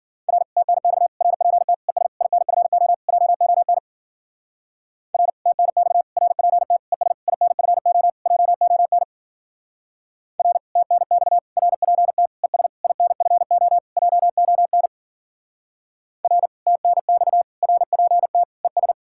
Morse code, heard as a keyed tone, about 52 wpm, 700 Hz. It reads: R TNX RPT ES INFO JON R TNX RPT ES INFO JON R TNX RPT ES INFO JON R TNX RPT ES